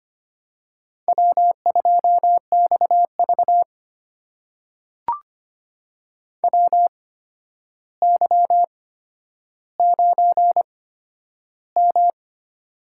Morse code, heard as a keyed tone, about 25 wpm, spaced as for 11 wpm, 700 Hz. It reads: W2XV WY9M
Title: USA Call Signs - 4 Characters - 8x Character Spacing - 25wpm